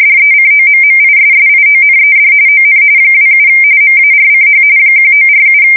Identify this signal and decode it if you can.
RTTY, about 45.45 baud. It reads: DECOMMISSIONED ON 27 MARCH 1947
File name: fsk170 / USS-Massachusetts